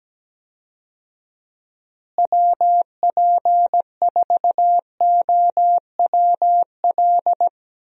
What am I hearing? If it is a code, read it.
Morse code, heard as a keyed tone, about 17 wpm, 700 Hz. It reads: WP4OWL
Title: USA Call Signs - 6 Characters - Repeated 3x - 17wpm